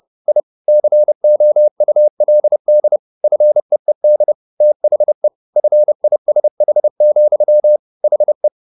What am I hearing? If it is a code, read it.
Morse code, heard as a keyed tone, about 30 wpm, 600 Hz. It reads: I COULD FEED THE FISH, HE
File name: the old man and the sea / the old man and the sea 1 30wpm